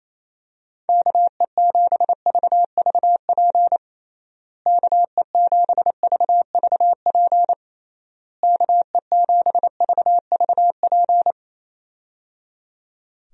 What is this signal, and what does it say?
Morse code, heard as a keyed tone, about 28 wpm, 700 Hz. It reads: KE7VVP KE7VVP KE7VVP
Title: USA Call Signs - 6 Characters - Repeated 3x - 28wpm